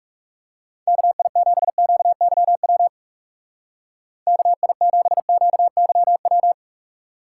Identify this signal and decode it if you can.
Morse code, heard as a keyed tone, about 37 wpm, 700 Hz. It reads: KI7QYW KI7QYW
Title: Speed Racing - USA 2x3 Call Signs 30wpm